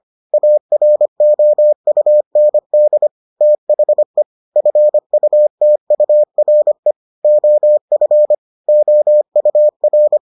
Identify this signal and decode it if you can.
Morse code, heard as a keyed tone, about 25 wpm, 600 Hz. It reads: AROUND THE FUTURE OF OUR